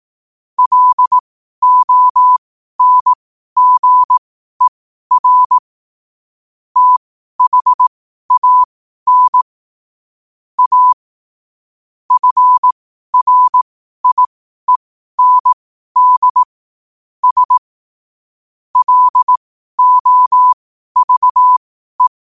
Morse code, heard as a keyed tone, about 18 wpm, spaced as for 12 wpm, 1000 Hz. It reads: LONGER THAN A FRIEND S LOVE
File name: qotdfiles / 12wpm